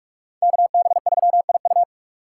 Morse code, heard as a keyed tone, about 39 wpm, 700 Hz. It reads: KB3IU